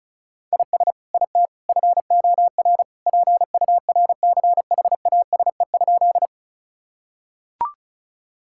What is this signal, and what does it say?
Morse code, heard as a keyed tone, about 36 wpm, 700 Hz. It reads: IS IT FOR PURCHASE?
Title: Sentences from Top 1000 Words - Review Part 5 - 35wpm